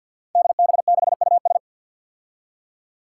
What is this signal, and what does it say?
Morse code, heard as a keyed tone, about 53 wpm, 700 Hz. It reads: DB6US
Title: Five Challenging Letter-Number Combinations 50wpm